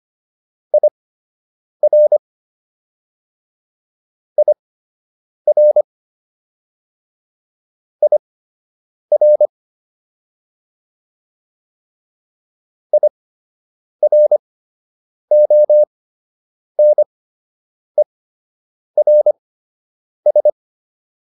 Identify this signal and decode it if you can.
Morse code, heard as a keyed tone, about 25 wpm, spaced as for 8 wpm, 600 Hz. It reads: IR IR IR IRONERS